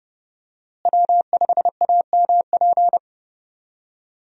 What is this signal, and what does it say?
Morse code, heard as a keyed tone, about 30 wpm, 700 Hz. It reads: W5AMP
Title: USA Call Signs - 5 Characters - Repeated 2x - 30wpm